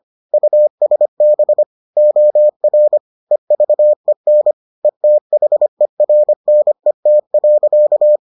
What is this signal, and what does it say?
Morse code, heard as a keyed tone, about 25 wpm, 600 Hz. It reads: USB OR EVEN ETHERNET.